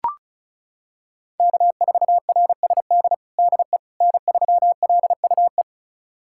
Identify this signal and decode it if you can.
Morse code, heard as a keyed tone, about 34 wpm, 700 Hz. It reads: K4RSD DE N3LUE
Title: USA Callsign DE Callsign - 5 Characters - 35wpm